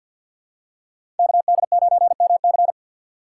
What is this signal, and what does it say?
Morse code, heard as a keyed tone, about 46 wpm, 700 Hz. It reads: KD9MC
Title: USA Call Signs - 5 Characters - 3x Character Spacing - 50wpm